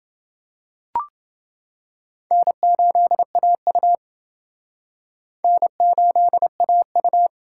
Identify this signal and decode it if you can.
Morse code, heard as a keyed tone, about 28 wpm, 700 Hz. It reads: N8AU N8AU